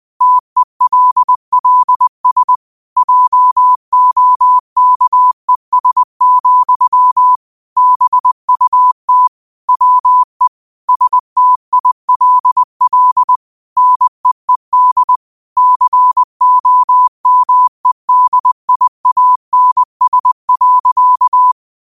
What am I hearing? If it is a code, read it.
Morse code, heard as a keyed tone, about 20 wpm, 1000 Hz. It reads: TELLS JOKES, BUT WE STILL NEED COMEDIANS.
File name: qotdfiles / 20wpm